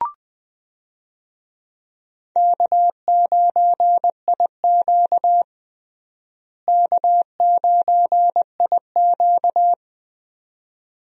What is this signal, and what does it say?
Morse code, heard as a keyed tone, about 21 wpm, 700 Hz. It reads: K9IQ K9IQ
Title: USA Call Signs - 4 Characters - Repeated 2x - 20wpm